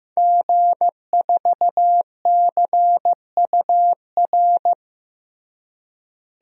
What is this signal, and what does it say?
Morse code, heard as a keyed tone, about 15 wpm, 700 Hz. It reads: G4CUR
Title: DXCC-100 Call Signs - 5 Characters - Repeated 2x - 15wpm